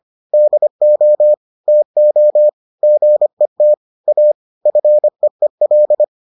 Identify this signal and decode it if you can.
Morse code, heard as a keyed tone, about 25 wpm, 600 Hz. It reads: DO TO GET A FEEL